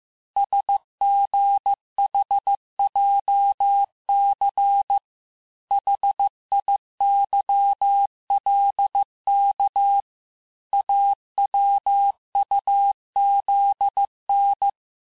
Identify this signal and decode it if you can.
Morse code, heard as a keyed tone, about 15 wpm, 800 Hz. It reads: SGHJC HIYLK AWUZN
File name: cours-du-lundi files / cours-1585-texte-chiffre-vitesse-14mpm